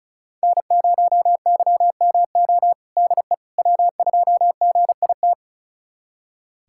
Morse code, heard as a keyed tone, about 34 wpm, 700 Hz. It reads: N0YMO DE W2GIT
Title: USA Callsign DE Callsign - 5 Characters - 35wpm